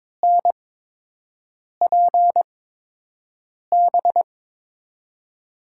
Morse code, heard as a keyed tone, about 22 wpm, spaced as for 6 wpm, 700 Hz. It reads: NPB
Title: USA Call Signs - 6 Characters - 8x Character Spacing - 22wpm